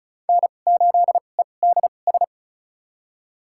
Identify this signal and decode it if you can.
Morse code, heard as a keyed tone, about 36 wpm, spaced as for 25 wpm, 700 Hz. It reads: N8EDS
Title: USA Call Signs - 5 Characters - 2x Character Spacing - 35wpm